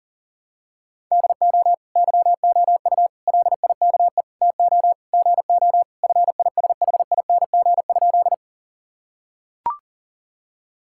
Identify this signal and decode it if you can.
Morse code, heard as a keyed tone, about 40 wpm, 700 Hz. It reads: DO YOU LIKE TO GO FISHING?
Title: Sentences from Top 1000 Words - Review Part 5 - 40wpm